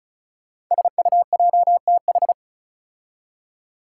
Morse code, heard as a keyed tone, about 36 wpm, 700 Hz. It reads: SUJTH
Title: Code Groups - 5 Letter:Numbers - 35wpm